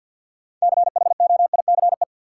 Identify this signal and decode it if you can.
Morse code, heard as a keyed tone, about 53 wpm, 700 Hz. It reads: CHOICE
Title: Sentences from Top 900 Words - New Word Drill - 50wpm